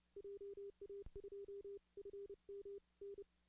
Morse code, heard as a keyed tone, about 29 wpm, 400 Hz. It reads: JA2FMN
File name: 2016 arrl dx cw / 2016-ARRL-DX-CW-K5TR- - 0-25